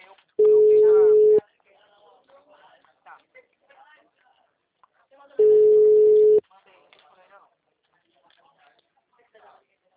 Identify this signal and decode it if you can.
Ringback tone (Europe, 425 Hz), with speech.